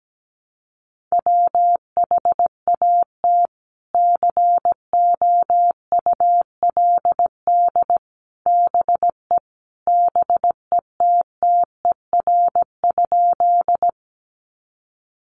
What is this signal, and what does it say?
Morse code, heard as a keyed tone, about 17 wpm, 700 Hz. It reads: WHAT COULD BE BETTER?
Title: Sentences from Top 400 Words 17wpm